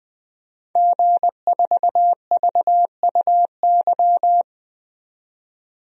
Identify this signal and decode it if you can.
Morse code, heard as a keyed tone, about 20 wpm, 700 Hz. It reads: G4VUY
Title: DXCC-100 Call Signs - 5 Characters - 20wpm